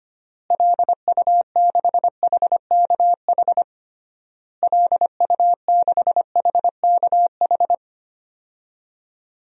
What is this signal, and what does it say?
Morse code, heard as a keyed tone, about 25 wpm, 700 Hz. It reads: LU6HKH LU6HKH
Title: DXCC-100 Call Signs - 6 Characters - Repeated 3x - 25wpm